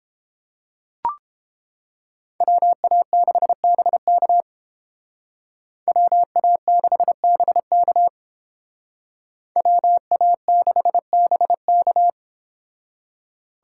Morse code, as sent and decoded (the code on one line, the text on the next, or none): .-- .- -.... -... -.- / .-- .- -.... -... -.- / .-- .- -.... -... -.-
WA6BK WA6BK WA6BK